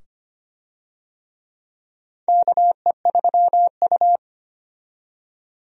Morse code, heard as a keyed tone, about 25 wpm, 700 Hz. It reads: KE3U
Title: USA Call Signs - 4 Characters - 25wpm